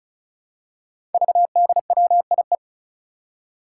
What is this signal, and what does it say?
Morse code, heard as a keyed tone, about 36 wpm, 700 Hz. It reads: VDWIE
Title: Five Challenging Letter-Number Combinations 35wpm